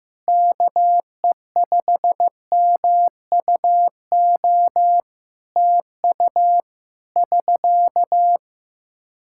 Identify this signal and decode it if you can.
Morse code, heard as a keyed tone, about 15 wpm, 700 Hz. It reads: KE5MUO TU <SK>
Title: K1USN SST QSO Practice 15wpm